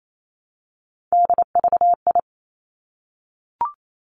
Morse code, heard as a keyed tone, about 28 wpm, 700 Hz. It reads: DVI
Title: Three Challenging Letter-Number Combinations 28wpm